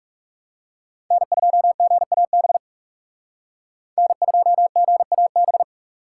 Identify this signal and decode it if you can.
Morse code, heard as a keyed tone, about 38 wpm, 700 Hz. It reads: N2GAB N2GAB